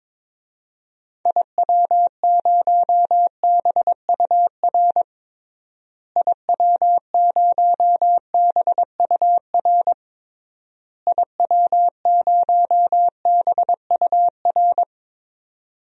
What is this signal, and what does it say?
Morse code, heard as a keyed tone, about 22 wpm, 700 Hz. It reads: IW0BUR IW0BUR IW0BUR
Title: DXCC-100 Call Signs - 6 Characters - Repeated 3x - 22wpm